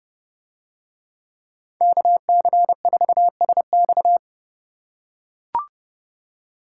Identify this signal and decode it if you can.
Morse code, heard as a keyed tone, about 28 wpm, 700 Hz. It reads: KC4SX